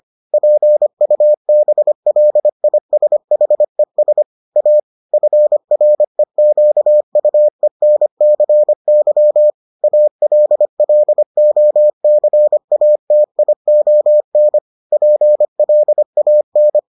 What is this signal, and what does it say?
Morse code, heard as a keyed tone, about 25 wpm, 600 Hz. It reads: PUBLISHES A FREQUENCY ALLOCATION PLAN